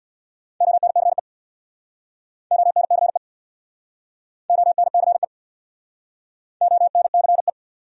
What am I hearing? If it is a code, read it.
Morse code, heard as a keyed tone, about 67 wpm, 700 Hz. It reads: ONCE ONCE ONCE ONCE